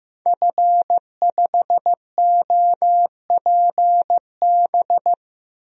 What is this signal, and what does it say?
Morse code, heard as a keyed tone, about 15 wpm, 700 Hz. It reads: F5OPB
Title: DXCC-100 Call Signs - 5 Characters - Repeated 2x - 15wpm